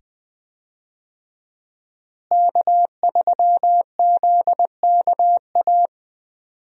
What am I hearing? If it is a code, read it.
Morse code, heard as a keyed tone, about 20 wpm, 700 Hz. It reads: K3ZKA
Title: USA Call Signs - 5 Characters - Repeated 2x - 20wpm